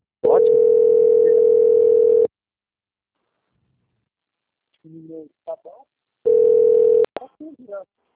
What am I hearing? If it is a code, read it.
Ringback tone (North America), 440 + 480 Hz, with speech.
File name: out- - - - - .3612